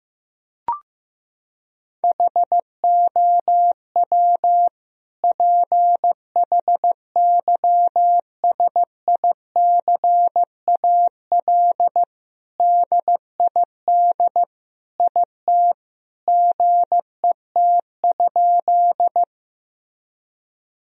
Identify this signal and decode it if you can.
Morse code, heard as a keyed tone, about 15 wpm, 700 Hz. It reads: HOW PHYSICAL DID IT GET?
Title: Sentences from Top 1000 Words - Review Part 4 - 15wpm